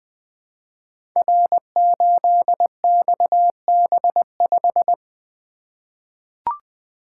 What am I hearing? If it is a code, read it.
Morse code, heard as a keyed tone, about 21 wpm, 700 Hz. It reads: R8XB5
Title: Code Groups - 5 Letter:Numbers - 20wpm